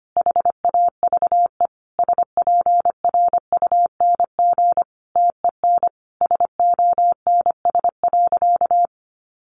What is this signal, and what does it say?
Morse code, heard as a keyed tone, about 25 wpm, 700 Hz. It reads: HAVE SPRUNG TEN SONS.
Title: A Sound Of Thunder 25wpm